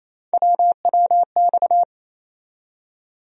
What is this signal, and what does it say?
Morse code, heard as a keyed tone, about 28 wpm, 700 Hz. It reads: WWX
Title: USA Call Sign Postfixes - 3 Letters 28wpm